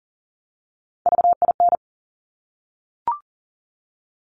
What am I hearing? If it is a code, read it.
Morse code, heard as a keyed tone, about 38 wpm, 690 Hz. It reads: VIN
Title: Top Three Letter Combinations in Words 40wpm